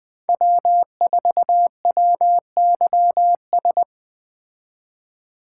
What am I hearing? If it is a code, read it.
Morse code, heard as a keyed tone, about 20 wpm, 700 Hz. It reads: W4WYS
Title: Speed Racing - USA 1x3 Call Signs 17wpm